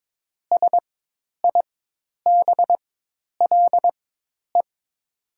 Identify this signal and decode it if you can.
Morse code, heard as a keyed tone, about 22 wpm, spaced as for 10 wpm, 690 Hz. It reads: SIBLE
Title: 4x Character Spacing - Top 500 Words - 22wpm